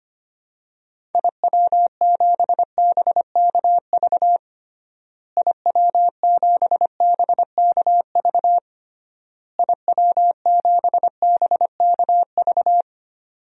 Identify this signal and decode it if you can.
Morse code, heard as a keyed tone, about 25 wpm, 700 Hz. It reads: IW7BKV IW7BKV IW7BKV
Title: DXCC-100 Call Signs - 6 Characters - Repeated 3x - 25wpm